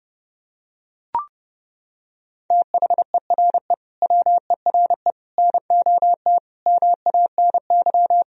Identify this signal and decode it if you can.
Morse code, heard as a keyed tone, about 32 wpm, 700 Hz. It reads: THERE WERE NOT MANY
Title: Sentences from Top 1000 Words - Review Part 1 - 30wpm